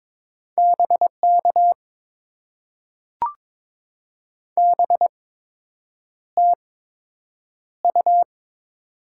Morse code, heard as a keyed tone, about 22 wpm, spaced as for 8 wpm, 700 Hz. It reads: BK BTU
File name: Top 90 QSO Elements - 8x Character Spacing - 22wpm